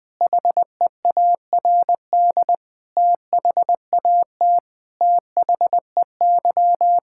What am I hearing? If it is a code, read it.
Morse code, heard as a keyed tone, about 20 wpm, 700 Hz. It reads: HEARD THAT THEY
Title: Sentences with 2-Word Phrases - 20wpm